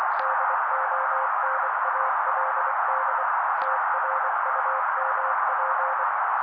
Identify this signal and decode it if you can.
Morse code, heard as a keyed tone, about 24 wpm, 530 Hz. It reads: DONALD TRUMP
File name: daily / 23wpm noise